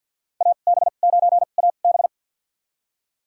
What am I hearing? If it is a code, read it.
Morse code, heard as a keyed tone, about 52 wpm, spaced as for 37 wpm, 700 Hz. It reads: AB9AB
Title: USA Call Signs - 5 Characters - 2x Character Spacing - 50wpm